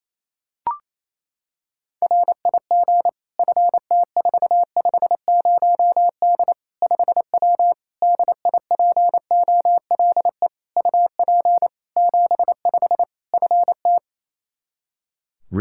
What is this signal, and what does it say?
Morse code, heard as a keyed tone, about 28 wpm, 700 Hz. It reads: RIG FT450D 5W DIPOLE UP 75 FT
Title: Practice QSO - Exchange 2 - Info - 28wpm